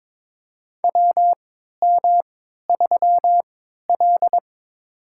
Morse code, heard as a keyed tone, about 21 wpm, spaced as for 12 wpm, 690 Hz. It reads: WM3L
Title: USA Call Signs - 4 Characters - 3x Character Spacing - 22wpm